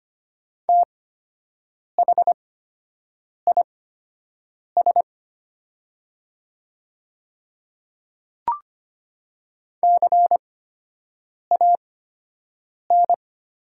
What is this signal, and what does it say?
Morse code, heard as a keyed tone, about 25 wpm, spaced as for 7 wpm, 690 Hz. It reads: THIS CAN
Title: 8x Character Spacing - Top 100 Words - 25wpm